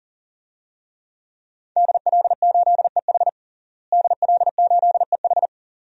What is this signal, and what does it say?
Morse code, heard as a keyed tone, about 40 wpm, 700 Hz. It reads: DL8EH DL8EH
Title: DXCC-100 Call Signs - 5 Characters - Repeated 2x - 40wpm